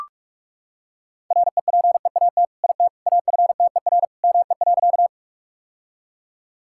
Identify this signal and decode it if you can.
Morse code, heard as a keyed tone, about 43 wpm, 700 Hz. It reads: REPEAT IT AFTER ME.
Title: Sentences from Top 1600 Words 45wpm